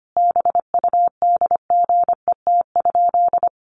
Morse code, heard as a keyed tone, about 25 wpm, 700 Hz. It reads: BUDGET?
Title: Sentences from Top 1200 Words 25wpm